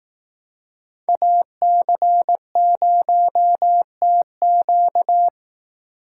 Morse code, heard as a keyed tone, about 18 wpm, 700 Hz. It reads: AC0TQ
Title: Speed Racing - USA 2x2 Call Signs 15wpm